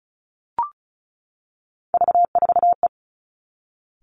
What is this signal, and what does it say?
Morse code, heard as a keyed tone, about 34 wpm, 700 Hz. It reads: V4E
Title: Three Challenging Letter-Number Combinations 35wpm